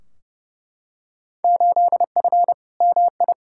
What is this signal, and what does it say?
Morse code, heard as a keyed tone, about 31 wpm, 700 Hz. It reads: 8F MI
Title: Field Day Exchanges Encore - 30wpm